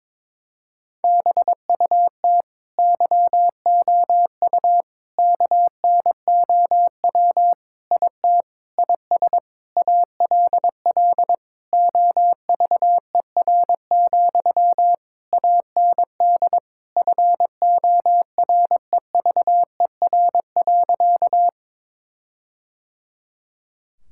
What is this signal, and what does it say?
Morse code, heard as a keyed tone, about 22 wpm, 700 Hz. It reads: BUT YOU KNOW IT IS ALL OVER, AND FOREVER.
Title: Sentences from Top 1600 Words 22wpm